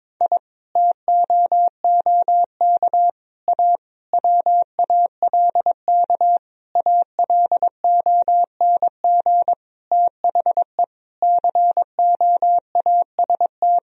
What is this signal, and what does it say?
Morse code, heard as a keyed tone, about 22 wpm, 700 Hz. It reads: I TOOK A WALK ALONG THE COAST
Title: Sentences from Top 1000 Words - Review Part 4 - 22wpm